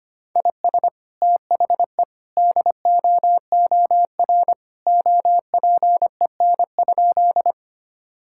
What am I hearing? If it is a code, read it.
Morse code, heard as a keyed tone, about 25 wpm, 700 Hz. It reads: IS THE DOOR OPEN?
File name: practice / 25wpm / Sentences with 2-Word Phrases - 25wpm